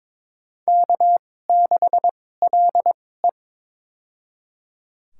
Morse code, heard as a keyed tone, about 22 wpm, spaced as for 16 wpm, 700 Hz. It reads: K6LE
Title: USA Call Signs - 4 Characters - 2x Character Spacing - 22wpm